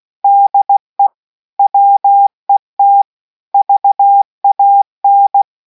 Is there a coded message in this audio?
Yes: Morse code from 0.2 to 5.4 s, heard as a keyed tone, about 16 wpm, 800 Hz. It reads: DE WET VAN